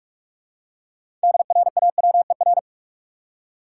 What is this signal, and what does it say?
Morse code, heard as a keyed tone, about 46 wpm, 700 Hz. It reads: DRAWER
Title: Six Letter Words - 45wpm